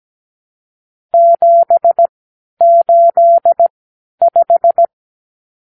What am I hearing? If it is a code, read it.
Morse code, heard as a keyed tone, about 17 wpm, spaced as for 10 wpm, 680 Hz. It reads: 785
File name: arrl / Review (1-0)